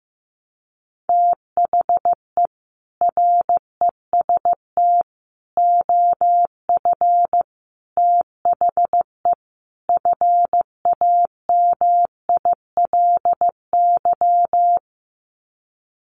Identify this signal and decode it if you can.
Morse code, heard as a keyed tone, about 14 wpm, 700 Hz. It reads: THE REST OF THE FAMILY
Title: Sets of 5 Words Encore2 15wpm